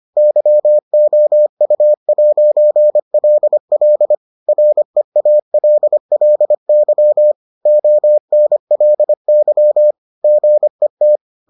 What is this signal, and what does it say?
Morse code, heard as a keyed tone, about 25 wpm, 600 Hz. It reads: YOU'LL REALLY ONLY GET